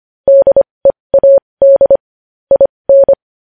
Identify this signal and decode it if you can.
Morse code, heard as a keyed tone, about 25 wpm, spaced as for 20 wpm, 550 Hz. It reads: DEAD IN
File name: podcast / News headlines-1x-25x20